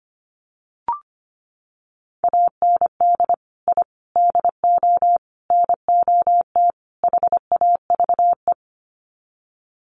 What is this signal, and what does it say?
Morse code, heard as a keyed tone, about 25 wpm, 700 Hz. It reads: AND I DO NOT HAVE
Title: Sets of 5 Words Encore 25wpm